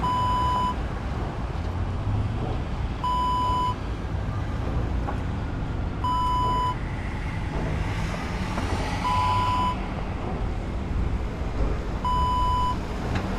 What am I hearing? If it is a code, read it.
There is no code.